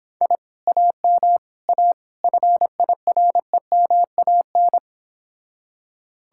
Morse code, heard as a keyed tone, about 25 wpm, 700 Hz. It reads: I AM A FIREMAN